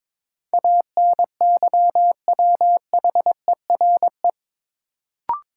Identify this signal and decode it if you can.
Morse code, heard as a keyed tone, about 22 wpm, 700 Hz. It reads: ANYWHERE
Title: Sentences from Top 1400 Words - New Word Drill - 22wpm